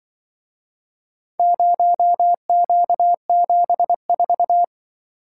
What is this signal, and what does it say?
Morse code, heard as a keyed tone, about 24 wpm, 700 Hz. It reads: 0Q74
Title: Speed-Racing - Four Letter-Number Combinations - 20wpm